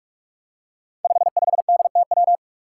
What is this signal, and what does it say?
Morse code, heard as a keyed tone, about 44 wpm, 700 Hz. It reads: 55DTW
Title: Five Challenging Letter-Number Combinations 45wpm